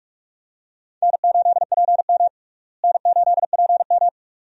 Morse code, heard as a keyed tone, about 46 wpm, 700 Hz. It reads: N8PM N8PM
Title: USA Call Signs - 4 Characters - Repeated 2x - 45wpm